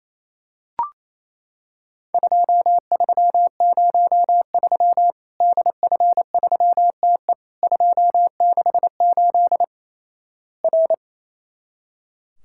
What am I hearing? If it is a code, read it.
Morse code, heard as a keyed tone, about 28 wpm, 700 Hz. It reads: 2303 DF3TE 268 R
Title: Worked All Europe DX Contest - QTC Exchange - 28wpm